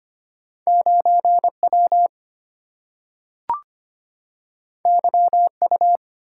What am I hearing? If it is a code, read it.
Morse code, heard as a keyed tone, about 25 wpm, 700 Hz. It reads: 9 W Y U
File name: Two Letter-Number Combinations - 25wpm